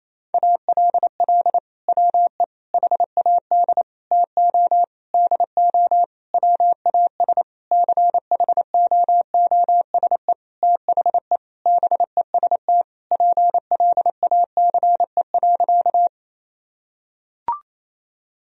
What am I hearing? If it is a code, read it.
Morse code, heard as a keyed tone, about 28 wpm, 700 Hz. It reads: ALL WE HAD TO DO WAS CHOOSE THE BEST PLACE.
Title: Sentences from Top 1000 Words - Review Part 4 - 28wpm